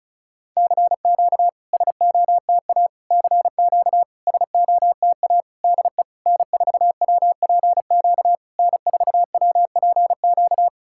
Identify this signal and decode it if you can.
Morse code, heard as a keyed tone, about 34 wpm, 690 Hz. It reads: CQ SOTA CQ SOTA DE N4WPQ N4WPQ
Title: SOTA QSO Practice 35wpm